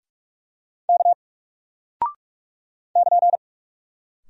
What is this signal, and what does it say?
Morse code, heard as a keyed tone, about 48 wpm, 700 Hz. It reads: K <KN>
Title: Single Character (Letter:Number:Punctuation:Prosign) Alt 45wpm